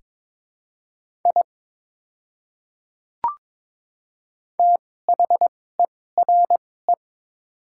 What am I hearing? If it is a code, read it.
Morse code, heard as a keyed tone, about 21 wpm, spaced as for 16 wpm, 700 Hz. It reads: I THERE
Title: 2x Character Spacing - Top 100 Words - 22wpm